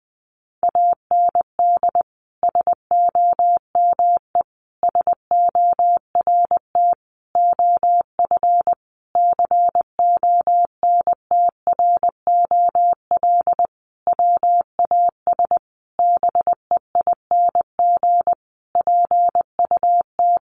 Morse code, heard as a keyed tone, about 20 wpm, 700 Hz. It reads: AND SOME SORT OF CONTROL WAS BEING PUT
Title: Sentences from Top 1300 Words 20wpm